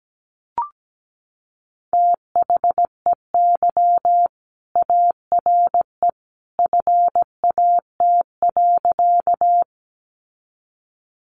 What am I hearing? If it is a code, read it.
Morse code, heard as a keyed tone, about 17 wpm, 700 Hz. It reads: THEY ARE FAT.
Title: Sentences from Top 600 Words 17wpm